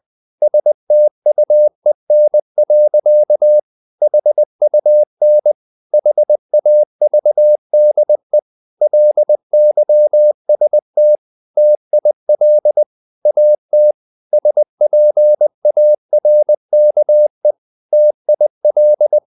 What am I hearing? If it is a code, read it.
Morse code, heard as a keyed tone, about 20 wpm, 600 Hz. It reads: STUEN. HUN HAVDE LYST TIL AT SPARKE TIL
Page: Ferie 19 wpm